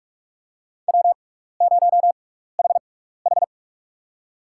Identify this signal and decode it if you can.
Morse code, heard as a keyed tone, about 43 wpm, spaced as for 15 wpm, 700 Hz. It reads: W0HH
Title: USA Call Signs - 4 Characters - 6x Character Spacing - 45wpm